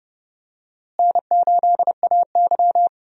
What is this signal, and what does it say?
Morse code, heard as a keyed tone, about 31 wpm, 700 Hz. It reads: N8AY
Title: USA Call Signs - 4 Characters - Repeated 3x - 30wpm